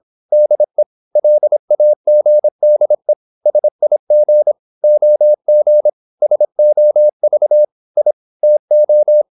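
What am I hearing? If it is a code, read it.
Morse code, heard as a keyed tone, about 26 wpm, 600 Hz. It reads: DE LAGDE SIG OG SOV I TO
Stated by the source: High-speed (QRQ) Morse at 26 wpm.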